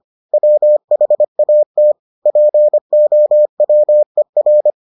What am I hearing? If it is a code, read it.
Morse code, heard as a keyed tone, about 25 wpm, 600 Hz. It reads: WHAT POWER